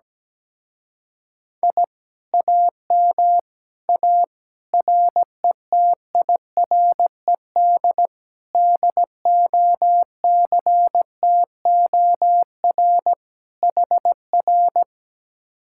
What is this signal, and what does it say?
Morse code, heard as a keyed tone, about 17 wpm, 700 Hz. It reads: I AM A RETIRED DOCTOR HR
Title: Speed-Racing Common QSO Phrases Encore - 17wpm